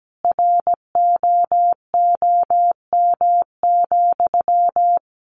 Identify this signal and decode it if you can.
Morse code, heard as a keyed tone, about 17 wpm, 700 Hz. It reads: ROOM,
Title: Sentences from Top 1400 Words 17wpm